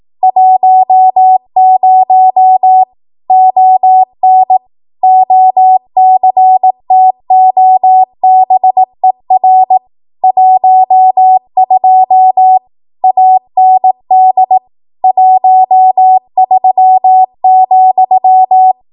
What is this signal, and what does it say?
Morse code, heard as a keyed tone, about 18 wpm, 750 Hz. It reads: 10 ON OCTOBER 12 AND 13,
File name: Archive / 18 WPM / 18WPM